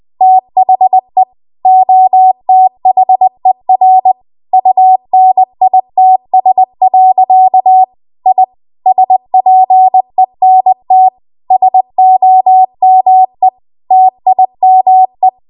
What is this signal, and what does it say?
Morse code, heard as a keyed tone, about 20 wpm, 750 Hz. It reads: THE OTHER UNITS. I SPENT SOME TIME
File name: Archive / 20 WPM / 20WPM